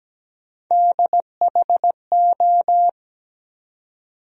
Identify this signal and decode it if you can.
Morse code, heard as a keyed tone, about 17 wpm, 700 Hz. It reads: DHO
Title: USA Call Sign Postfixes - 3 Letters 17wpm